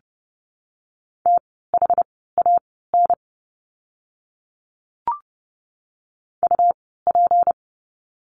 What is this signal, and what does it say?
Morse code, heard as a keyed tone, about 30 wpm, spaced as for 17 wpm, 700 Hz. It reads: THAN UP
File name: course / 30:10wpm / 230 Lesson 37f - Letter x - Words 30wpm